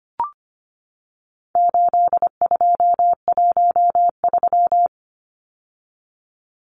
Morse code, heard as a keyed tone, about 25 wpm, 700 Hz. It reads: 8213
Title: Numbers - 4-Digit 25wpm